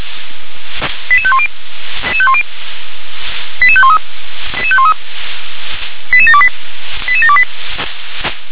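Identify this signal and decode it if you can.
Five-tone selective call (ZVEI), heard as quick runs of five tones, tones of about 70 ms. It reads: 88510 88513 88728 88728
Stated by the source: ZVEI five-tone selective call received on an SDR.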